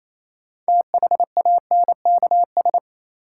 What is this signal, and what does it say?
Morse code, heard as a keyed tone, about 28 wpm, 700 Hz. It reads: THANKS
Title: Sentences from Top 600 Words - New Word Drill - 28wpm